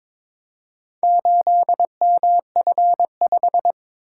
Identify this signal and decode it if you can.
Morse code, heard as a keyed tone, about 23 wpm, 700 Hz. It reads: 8MF5
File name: practice / 22wpm / Four Letter-Number Combinations - 22wpm